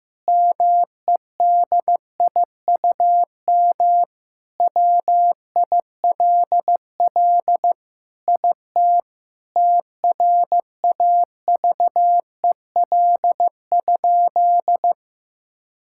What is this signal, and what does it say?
Morse code, heard as a keyed tone, about 14 wpm, 700 Hz. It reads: MEDIUM WILL IT TRAVEL?
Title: Sentences from Top 1000 Words - Review Part 6 - 15wpm